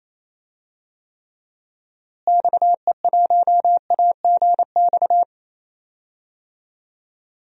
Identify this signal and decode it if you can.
Morse code, heard as a keyed tone, about 28 wpm, 700 Hz. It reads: XE1AGX
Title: DXCC-100 Call Signs - 6 Characters - Repeated 2x - 28wpm